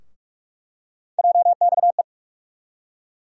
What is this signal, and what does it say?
Morse code, heard as a keyed tone, about 46 wpm, 700 Hz. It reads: JXE
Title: USA Call Sign Postfixes - 3 Letters 45wpm